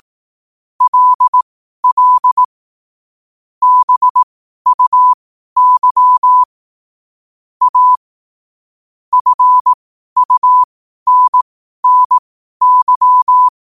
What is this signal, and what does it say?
Morse code, heard as a keyed tone, about 18 wpm, spaced as for 12 wpm, 1000 Hz. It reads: LL BUY A FUNNY
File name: qotdfiles / 12wpm